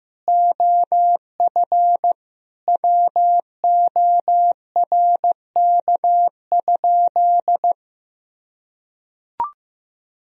Morse code, heard as a keyed tone, about 16 wpm, 700 Hz. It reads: OF WORK?
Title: Sentences from Top 1000 Words - Review Part 2 - 15wpm